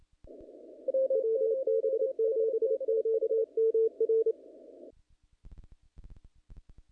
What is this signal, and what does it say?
Morse code, heard as a keyed tone, about 27 wpm, 450 Hz. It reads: WB7QMR
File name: contests / naqp-cw-k5tr-op-k5pi- -1900